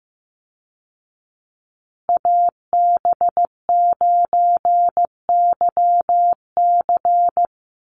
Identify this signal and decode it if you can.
Morse code, heard as a keyed tone, about 15 wpm, 700 Hz. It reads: AB9YC
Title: Call Signs Repeated 2x - 15wpm